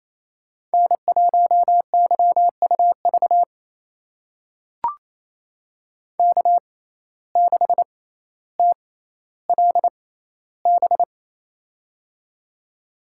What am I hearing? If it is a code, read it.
Morse code, heard as a keyed tone, about 28 wpm, spaced as for 14 wpm, 700 Hz. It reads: N1YUV K6TLB
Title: USA Call Signs - 5 Characters - 6x Character Spacing - 28wpm